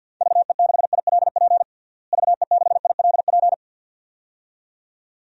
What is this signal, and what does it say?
Morse code, heard as a keyed tone, about 53 wpm, 700 Hz. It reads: VE6ILP VE6ILP
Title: DXCC-100 Call Signs - 6 Characters - Repeated 2x - 50wpm